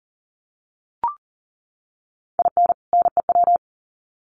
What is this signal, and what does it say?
Morse code, heard as a keyed tone, about 40 wpm, 700 Hz. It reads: IN NEW
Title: Sets of 2 Words 40wpm